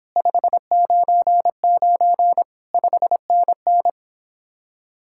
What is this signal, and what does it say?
Morse code, heard as a keyed tone, about 26 wpm, 700 Hz. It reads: 599 5NN
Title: Speed-Racing Common QSO Phrases - 22wpm